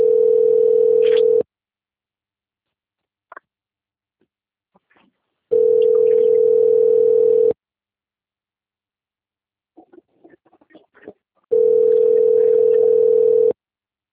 Ringback tone (North America), 440 + 480 Hz.
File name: out- - - - - .79828